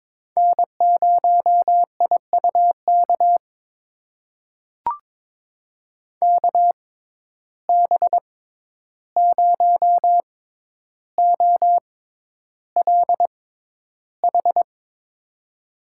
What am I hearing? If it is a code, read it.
Morse code, heard as a keyed tone, about 22 wpm, spaced as for 11 wpm, 700 Hz. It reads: N0IUK KB0OLH
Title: USA Call Signs - 6x Character Spacing - 22wpm